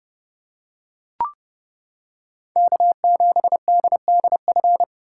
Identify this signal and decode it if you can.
Morse code, heard as a keyed tone, about 29 wpm, 700 Hz. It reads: K7DDF